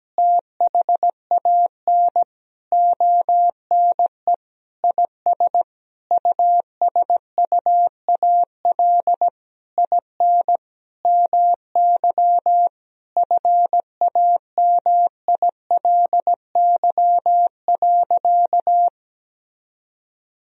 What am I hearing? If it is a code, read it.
Morse code, heard as a keyed tone, about 17 wpm, 700 Hz. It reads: THAN ONE IS USUAL IN MY FAMILY.
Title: Sentences from Top 1000 Words - Review Part 2 - 17wpm